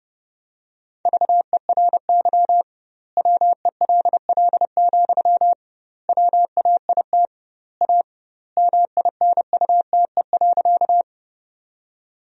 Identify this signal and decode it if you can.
Morse code, heard as a keyed tone, about 30 wpm, 700 Hz. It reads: VERY WELL, WAIT A MINUTE.
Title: 2x Word-Spacing - Sentences from Top 500 Words - 30wpm